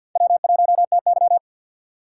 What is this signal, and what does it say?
Morse code, heard as a keyed tone, about 52 wpm, 700 Hz. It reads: W1TY
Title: USA Call Signs - 4 Characters - 2x Character Spacing - 50wpm